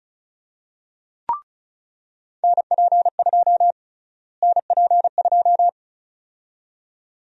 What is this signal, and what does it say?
Morse code, heard as a keyed tone, about 36 wpm, 700 Hz. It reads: NP2 NP2
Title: USA Call Sign Prefixes - Repeated 2x - 35wpm